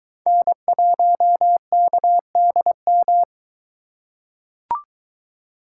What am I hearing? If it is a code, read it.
Morse code, heard as a keyed tone, about 23 wpm, 700 Hz. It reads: N1KDM